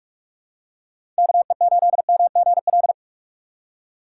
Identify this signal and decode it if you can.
Morse code, heard as a keyed tone, about 42 wpm, 700 Hz. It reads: KE8MGL
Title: USA Call Signs - 3x Character Spacing - 45wpm